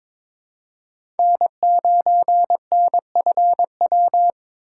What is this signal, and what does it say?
Morse code, heard as a keyed tone, about 22 wpm, 700 Hz. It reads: N9NFW
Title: USA Call Signs - 5 Characters - 3x Character Spacing - 22wpm